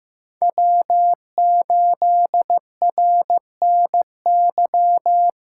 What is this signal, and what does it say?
Morse code, heard as a keyed tone, about 15 wpm, 700 Hz. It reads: W8RNY
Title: USA Callsign DE Callsign 15wpm